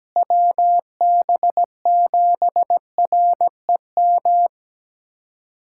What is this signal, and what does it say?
Morse code, heard as a keyed tone, about 17 wpm, 700 Hz. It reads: WB7REM
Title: USA Callsign DE Callsign - 6 Characters - 17wpm